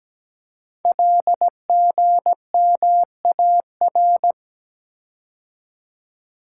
Morse code, heard as a keyed tone, about 17 wpm, 700 Hz. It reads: LGMAR
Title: Code Groups - 5 Letters - 17wpm